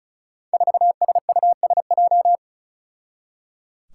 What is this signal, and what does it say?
Morse code, heard as a keyed tone, about 36 wpm, 700 Hz. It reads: 4SUSJ
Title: Five Challenging Letter-Number Combinations 35wpm